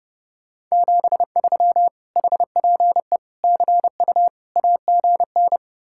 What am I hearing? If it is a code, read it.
Morse code, heard as a keyed tone, about 35 wpm, 700 Hz. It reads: 73 HPE CU AGN
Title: Speed-Racing Common QSO Phrases Encore - 30wpm